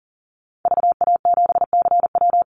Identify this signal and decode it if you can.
Morse code, heard as a keyed tone, about 40 wpm, 700 Hz. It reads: VA7CW
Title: Call Signs Repeated 3x - 40wpm